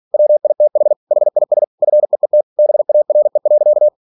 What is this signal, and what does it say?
Morse code, heard as a keyed tone, about 46 wpm, 600 Hz. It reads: WITH HIS FEET BARE.
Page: Never 47wpm